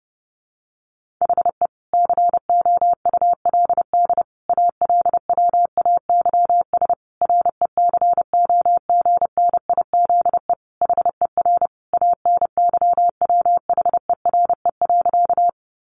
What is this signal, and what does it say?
Morse code, heard as a keyed tone, about 28 wpm, 700 Hz. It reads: HE COULD ALWAYS RECOGNIZE HER ANYWHERE.
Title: Sentences from Top 1400 Words 30wpm